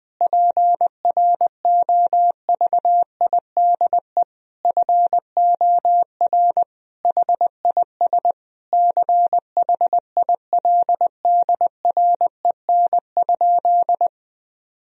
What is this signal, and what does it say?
Morse code, heard as a keyed tone, about 20 wpm, 700 Hz. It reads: PROVIDE FOR HIS CHILDREN?
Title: Sentences from Top 1000 Words - Review Part 3 - 20wpm